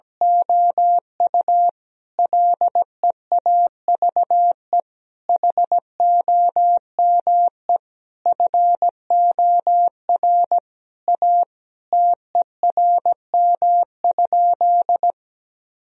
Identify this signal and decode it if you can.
Morse code, heard as a keyed tone, about 17 wpm, 700 Hz. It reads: OU LEAVE HOME FOR A TERM?
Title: Sentences from Top 1000 Words - Review Part 4 - 17wpm